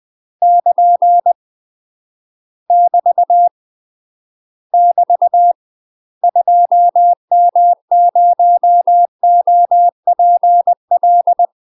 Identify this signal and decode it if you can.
Morse code, heard as a keyed tone, about 20 wpm, 700 Hz. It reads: <KN> <BT> <BT> 2M0OPL
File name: UK MASTER QSOs TRAINING / 20WPM / UK MASTER QSOs 0052